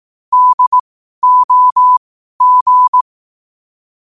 Morse code, heard as a keyed tone, about 18 wpm, spaced as for 12 wpm, 1000 Hz. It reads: DOG
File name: qotdfiles / 12wpm